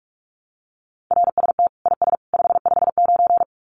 Morse code, heard as a keyed tone, about 44 wpm, 700 Hz. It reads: RST IS 559
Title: Common QSO Phrases 45wpm